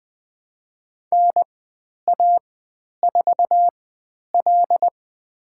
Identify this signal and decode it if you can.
Morse code, heard as a keyed tone, about 20 wpm, spaced as for 10 wpm, 700 Hz. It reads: NA4L
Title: 095 Lesson 16g - Letter c - Call Signs 20wpm